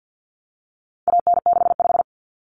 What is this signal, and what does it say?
Morse code, heard as a keyed tone, about 50 wpm, 700 Hz. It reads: AN65
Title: Common Cut Numbers - 4-Digit 50wpm